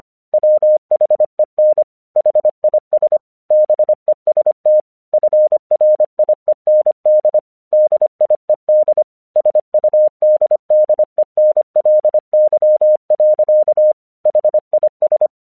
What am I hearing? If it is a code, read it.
Morse code, heard as a keyed tone, about 25 wpm, 610 Hz. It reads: WHEN HIS BEST FRIEND DIED SUDDENLY. HIS
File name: 1 English 25wpm / AQC12